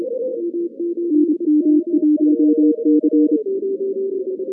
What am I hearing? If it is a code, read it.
Morse code, heard as a keyed tone, about 27 wpm, 310 Hz. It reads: NM1C
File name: 2015 cq ww cw / 2015 cqwww k5tr - 0-18